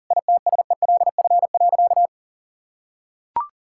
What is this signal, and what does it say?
Morse code, heard as a keyed tone, about 40 wpm, 700 Hz. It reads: ITSELF.
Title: Sentences from Top 1000 Words - Review Part 5 - 40wpm